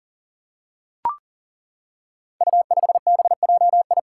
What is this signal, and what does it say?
Morse code, heard as a keyed tone, about 40 wpm, 700 Hz. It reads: U5BJI
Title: Five Challenging Letter-Number Combinations 40wpm